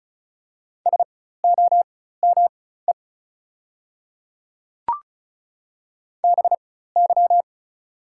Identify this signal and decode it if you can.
Morse code, heard as a keyed tone, about 36 wpm, spaced as for 16 wpm, 690 Hz. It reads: SOME BY
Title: 4x Character Spacing - Top 100 Words - 35wpm